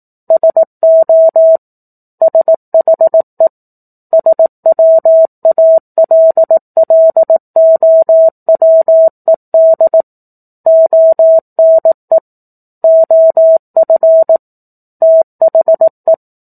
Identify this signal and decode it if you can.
Morse code, heard as a keyed tone, about 18 wpm, 650 Hz. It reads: SO SHE SWALLOWED ONE OF THE
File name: 04-Chapter-04 18 WPM